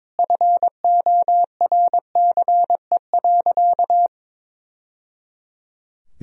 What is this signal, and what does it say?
Morse code, heard as a keyed tone, about 22 wpm, 700 Hz. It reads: FORCE.
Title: Sentences from Top 1000 Words - Review Part 4 - 22wpm